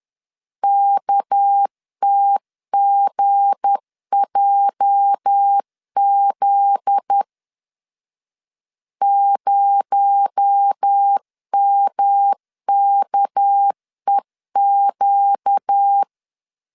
Morse code, heard as a keyed tone, about 10 wpm, 800 Hz. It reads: KTGJZ 0MKEQ
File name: LEZIONI / 9 286 10wpm 2735